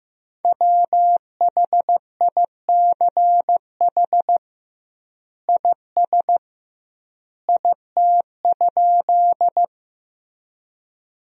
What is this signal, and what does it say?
Morse code, heard as a keyed tone, about 16 wpm, 700 Hz. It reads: WHICH IS IT?
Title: 2x Word-Spacing - Sentences from Top 100 Words - 15wpm